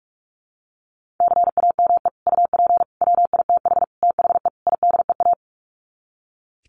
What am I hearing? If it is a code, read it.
Morse code, heard as a keyed tone, about 44 wpm, 700 Hz. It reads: CAME UP WITH THE IDEA
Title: Sets of 5 Words Encore2 45wpm